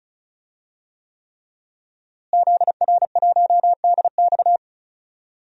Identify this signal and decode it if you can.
Morse code, heard as a keyed tone, about 34 wpm, 700 Hz. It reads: ZR1DX
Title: DXCC-100 Call Signs - 5 Characters - Repeated 3x - 35wpm